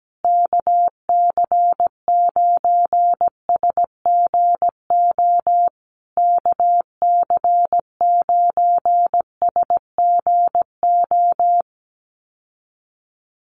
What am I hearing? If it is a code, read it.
Morse code, heard as a keyed tone, about 17 wpm, 700 Hz. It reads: KC9SGO KC9SGO
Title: USA Call Signs Repeated 3x 17wpm